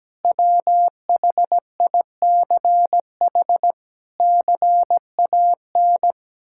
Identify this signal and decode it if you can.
Morse code, heard as a keyed tone, about 17 wpm, 700 Hz. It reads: WHICH CAN
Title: Sentences from Top 1600 Words 17wpm